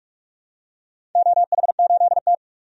Morse code, heard as a keyed tone, about 46 wpm, 700 Hz. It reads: OH8T
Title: DXCC-100 Call Signs - 4 Characters - 45wpm